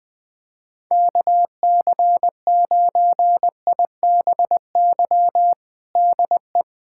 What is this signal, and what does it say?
Morse code, heard as a keyed tone, about 21 wpm, 700 Hz. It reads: KC9IBY DE
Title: USA Callsign DE Callsign - 6 Characters - 20wpm